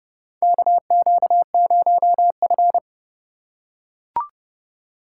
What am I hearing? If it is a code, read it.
Morse code, heard as a keyed tone, about 30 wpm, 700 Hz. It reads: KQ0F
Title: USA Call Signs - 4 Characters - 2x Character Spacing - 30wpm